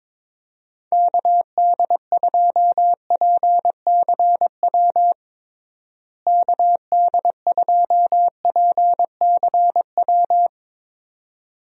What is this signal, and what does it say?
Morse code, heard as a keyed tone, about 22 wpm, 700 Hz. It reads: KD2PCW KD2PCW
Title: USA Call Signs - 6 Characters - Repeated 2x - 22wpm